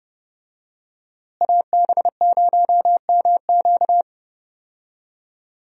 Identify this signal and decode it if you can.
Morse code, heard as a keyed tone, about 30 wpm, 700 Hz. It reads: AB0MQ
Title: USA Call Signs - 5 Characters - Repeated 2x - 30wpm